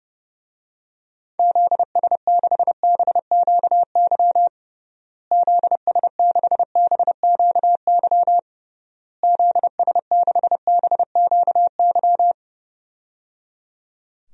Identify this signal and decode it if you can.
Morse code, heard as a keyed tone, about 30 wpm, 700 Hz. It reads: ZS6BQY ZS6BQY ZS6BQY
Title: DXCC-100 Call Signs - 6 Characters - Repeated 3x - 30wpm